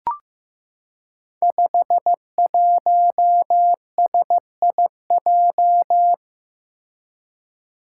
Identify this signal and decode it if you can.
Morse code, heard as a keyed tone, about 15 wpm, 700 Hz. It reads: 51SIJ
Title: Five Challenging Letter-Number Combinations 15wpm